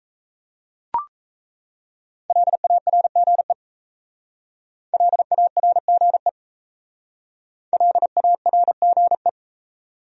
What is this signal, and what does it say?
Morse code, heard as a keyed tone, about 34 wpm, 700 Hz. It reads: LARGE LARGE LARGE
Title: Speed-Racing Top 200 Words 28wpm